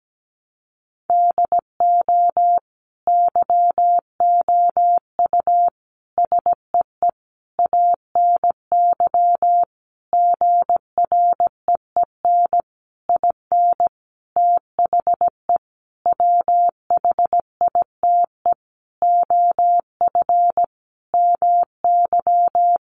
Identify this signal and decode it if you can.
Morse code, heard as a keyed tone, about 17 wpm, 700 Hz. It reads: DO YOU SEE ANY GREEN IN THE WHITE OF MY
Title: Sentences from Top 500 Words 17wpm